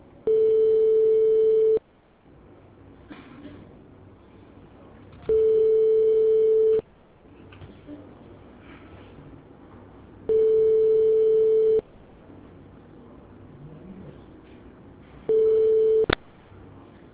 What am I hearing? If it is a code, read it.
Telephone call progress tone (France), heard as a telephone network's tone, about 440 Hz. Ringback tone.